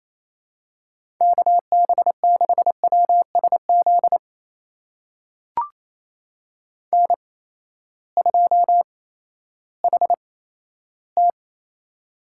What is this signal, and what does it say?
Morse code, heard as a keyed tone, about 28 wpm, spaced as for 14 wpm, 700 Hz. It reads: KB6WSZ N2HT
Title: USA Call Signs - 8x Character Spacing - 28wpm